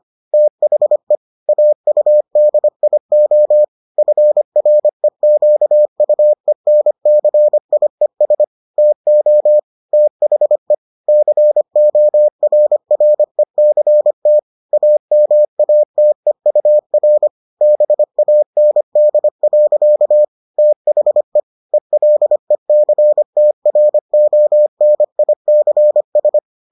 Morse code, heard as a keyed tone, about 25 wpm, 600 Hz. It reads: THE AUDIO FREQUENCIES TO THE CORRECT AMATEUR BAND. THE ELECTRONICS